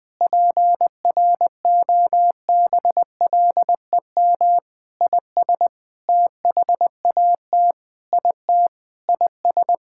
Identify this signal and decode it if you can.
Morse code, heard as a keyed tone, about 19 wpm, 700 Hz. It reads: PROBLEM IS THAT IT IS
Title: Sentences with 6-Word Phrases - Spoken Beginning - 20wpm